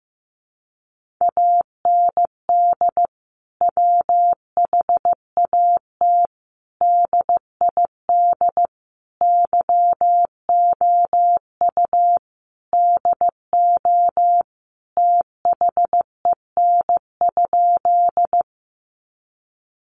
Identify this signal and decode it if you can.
Morse code, heard as a keyed tone, about 15 wpm, 700 Hz. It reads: AND WHAT DID YOU DO THEN?
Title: Sentences from Top 100 Words 15wpm